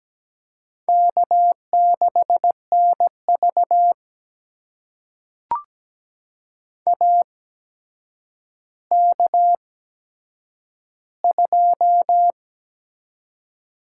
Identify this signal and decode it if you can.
Morse code, heard as a keyed tone, about 17 wpm, spaced as for 8 wpm, 700 Hz. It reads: K6NV AK2
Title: USA Call Signs - 4 Characters - 8x Character Spacing - 17wpm